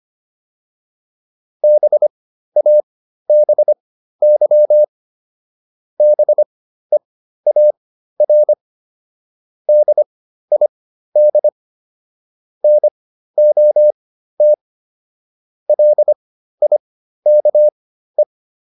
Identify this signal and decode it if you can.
Morse code, heard as a keyed tone, about 25 wpm, spaced as for 13 wpm, 600 Hz. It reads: BABY BEAR DID NOT LIKE